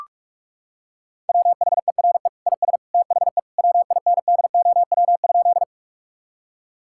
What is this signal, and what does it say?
Morse code, heard as a keyed tone, about 46 wpm, 700 Hz. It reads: WHERE IS THE WINDOW?
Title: Sentences from Top 1600 Words 45wpm